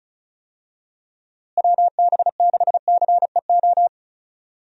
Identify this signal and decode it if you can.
Morse code, heard as a keyed tone, about 36 wpm, 700 Hz. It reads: WB6CEO
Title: USA Call Signs - 6 Characters - Repeated 3x - 35wpm